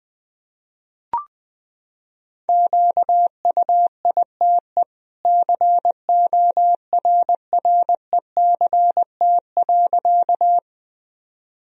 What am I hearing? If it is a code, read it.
Morse code, heard as a keyed tone, about 20 wpm, 700 Hz. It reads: QUITE CORRECT.
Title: Sentences from Top 1000 Words - Review Part 2 - 20wpm